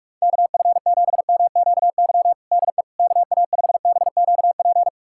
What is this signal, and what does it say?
Morse code, heard as a keyed tone, about 42 wpm, 700 Hz. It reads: KF7MQY DE KA5BQP
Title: USA Callsign DE Callsign - 6 Characters - 45wpm